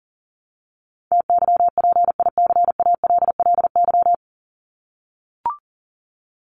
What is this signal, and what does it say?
Morse code, heard as a keyed tone, about 40 wpm, 700 Hz. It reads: TYPICALLY
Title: LY Suffix - Top 100 Words with Base Words 40wpm